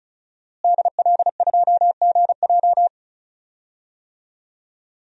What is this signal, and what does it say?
Morse code, heard as a keyed tone, about 36 wpm, 700 Hz. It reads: DL2GJ
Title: DXCC-100 Call Signs - 5 Characters - Repeated 2x - 35wpm